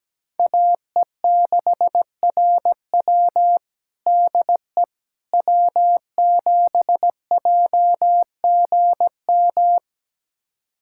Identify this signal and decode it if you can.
Morse code, heard as a keyed tone, about 17 wpm, 700 Hz. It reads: AE6RW DE W7JGM
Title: USA Callsign DE Callsign - 5 Characters - 17wpm